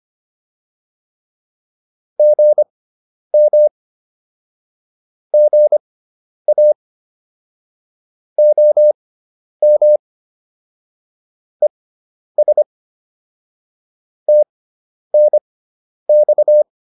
Morse code, heard as a keyed tone, about 25 wpm, spaced as for 10 wpm, 600 Hz. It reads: GM GA OM ES TNX
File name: QSO 410 10